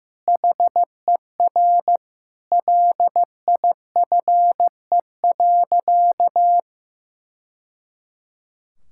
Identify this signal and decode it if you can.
Morse code, heard as a keyed tone, about 15 wpm, 700 Hz. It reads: HER LIFE.
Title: Sentences from Top 1000 Words - Review Part 6 - 15wpm